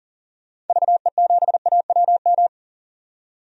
Morse code, heard as a keyed tone, about 39 wpm, 700 Hz. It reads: VE7AWM
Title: DXCC-100 Call Signs - 6 Characters - 40wpm